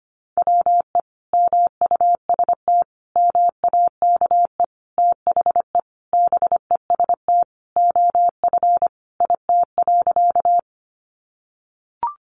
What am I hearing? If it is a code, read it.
Morse code, heard as a keyed tone, about 25 wpm, 700 Hz. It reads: WE MUST MAKE THE BEST OF IT.
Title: Sentences from Top 400 Words 25wpm